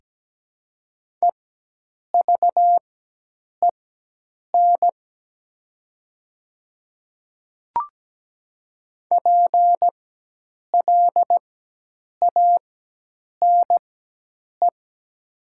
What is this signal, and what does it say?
Morse code, heard as a keyed tone, about 17 wpm, spaced as for 8 wpm, 700 Hz. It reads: EVEN PLANE
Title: 4x Character Spacing - Top 500 Words - 17wpm